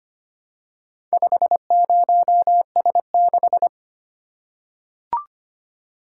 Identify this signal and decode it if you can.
Morse code, heard as a keyed tone, about 25 wpm, 700 Hz. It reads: 50S6
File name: practice / 25wpm / Four Letter-Number Combinations - 25wpm